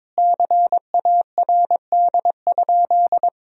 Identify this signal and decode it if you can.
Morse code, heard as a keyed tone, about 22 wpm, 700 Hz. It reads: CARD?
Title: Sentences from Top 1000 Words - Review Part 4 - 22wpm